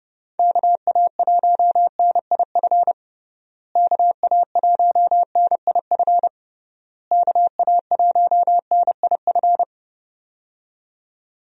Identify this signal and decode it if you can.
Morse code, heard as a keyed tone, about 32 wpm, 700 Hz. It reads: KA1NIF KA1NIF KA1NIF
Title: USA Call Signs - 6 Characters - Repeated 3x - 30wpm